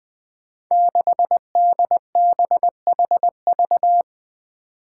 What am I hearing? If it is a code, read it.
Morse code, heard as a keyed tone, about 21 wpm, 700 Hz. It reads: 6DBHV
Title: Five Challenging Letter-Number Combinations 20wpm